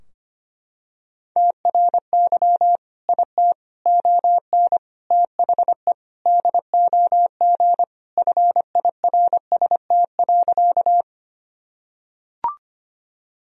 Morse code, heard as a keyed tone, about 25 wpm, 700 Hz. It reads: TRY IT ON THE DOG FIRST.
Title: Sentences from Top 1000 Words - Review Part 2 - 25wpm